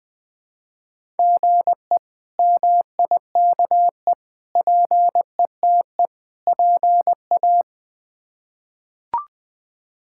Morse code, heard as a keyed tone, about 20 wpm, 700 Hz. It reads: GE MIKE PETE PA
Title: K1USN SST QSO Practice - 20wpm